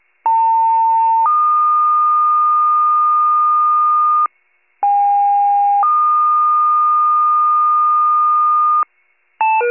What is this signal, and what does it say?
Two-tone page, heard as a one-second tone, then a three-second tone, about 900 then 1220 Hz, about 810 then 1220 Hz. It is heard from 0.2 to 8.8 s.